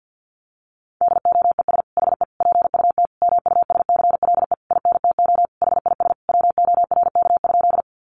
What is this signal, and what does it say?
Morse code, heard as a keyed tone, about 50 wpm, 700 Hz. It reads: DOES HE PUT MUSCLE INTO HIS WORK?
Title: Sentences from Top 1300 Words 50wpm